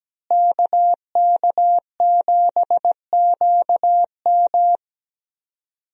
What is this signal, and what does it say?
Morse code, heard as a keyed tone, about 17 wpm, 700 Hz. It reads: KK7QM
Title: USA Call Signs - 5 Characters - Repeated 3x - 17wpm